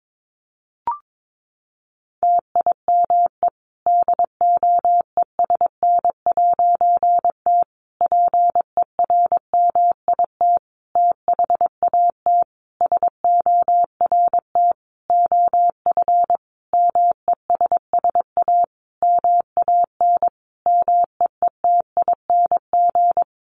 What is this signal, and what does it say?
Morse code, heard as a keyed tone, about 22 wpm, 700 Hz. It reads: TIME DOESN'T PERMIT THAT SORT OF MESS A MAN MEETING
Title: A Sound Of Thunder 22wpm